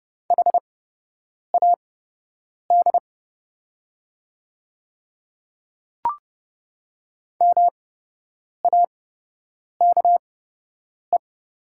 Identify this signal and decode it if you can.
Morse code, heard as a keyed tone, about 29 wpm, spaced as for 8 wpm, 700 Hz. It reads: HAD MAKE
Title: 8x Character Spacing - Top 100 Words - 30wpm